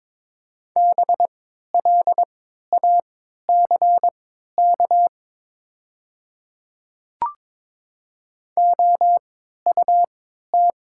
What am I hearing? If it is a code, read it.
Morse code, heard as a keyed tone, about 22 wpm, spaced as for 12 wpm, 700 Hz. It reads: BLACK OUT
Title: 3x Character Spacing - Top 500 Words - 22wpm